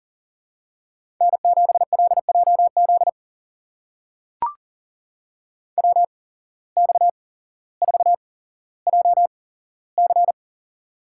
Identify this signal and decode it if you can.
Morse code, heard as a keyed tone, about 40 wpm, spaced as for 17 wpm, 700 Hz. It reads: N7LJZ WX4JC
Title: USA Call Signs - 5 Characters - 8x Character Spacing - 40wpm